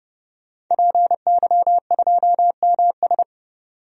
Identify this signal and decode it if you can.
Morse code, heard as a keyed tone, about 32 wpm, 700 Hz. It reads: PY2MS
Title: DXCC-100 Call Signs - 5 Characters - 30wpm